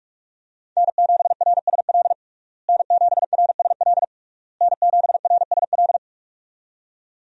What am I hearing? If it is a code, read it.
Morse code, heard as a keyed tone, about 41 wpm, 700 Hz. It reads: N7RSL N7RSL N7RSL
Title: USA Call Signs - 5 Characters - Repeated 3x - 45wpm